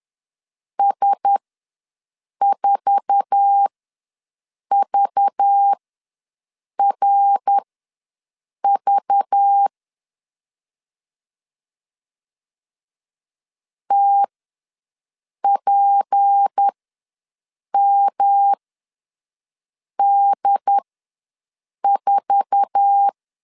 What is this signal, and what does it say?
Morse code, heard as a keyed tone, about 10 wpm, spaced as for 6 wpm, 800 Hz. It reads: S4VRV TPMD4
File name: LEZIONI / 8 YJ45 / 8 YJ4 ws 10wpm 4020